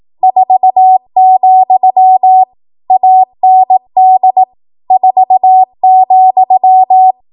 Morse code, heard as a keyed tone, about 18 wpm, 750 Hz. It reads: 4, AND 4,
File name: Morse / 18WPM